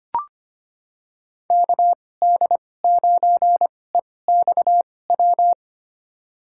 Morse code, heard as a keyed tone, about 25 wpm, spaced as for 18 wpm, 700 Hz. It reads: KD9EXW
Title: USA Call Signs - 6 Characters - 2x Character Spacing - 25wpm